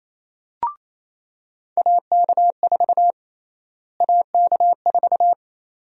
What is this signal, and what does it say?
Morse code, heard as a keyed tone, about 28 wpm, 700 Hz. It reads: AK4 AK4
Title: USA Call Sign Prefixes - Repeated 2x - 28wpm